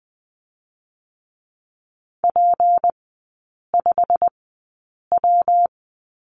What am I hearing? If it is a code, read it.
Morse code, heard as a keyed tone, about 20 wpm, spaced as for 8 wpm, 700 Hz. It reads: P5W
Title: 218 Lesson 35e - Number 0 - Three Characters 20wpm